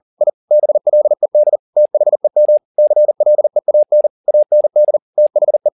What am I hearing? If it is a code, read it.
Morse code, heard as a keyed tone, about 40 wpm, 600 Hz. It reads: I BLED THEM CLEAN AND THE
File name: the old man and the sea 2 40wpm